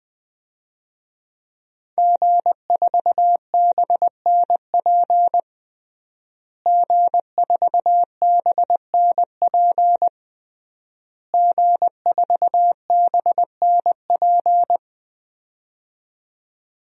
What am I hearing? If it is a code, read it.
Morse code, heard as a keyed tone, about 20 wpm, 700 Hz. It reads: G4BNP G4BNP G4BNP
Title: DXCC-100 Call Signs - 5 Characters - Repeated 3x - 20wpm